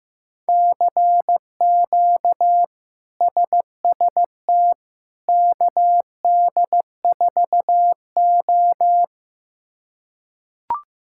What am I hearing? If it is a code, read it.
Morse code, heard as a keyed tone, about 15 wpm, 700 Hz. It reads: CQ SST KD4O
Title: K1USN SST QSO Practice - 15wpm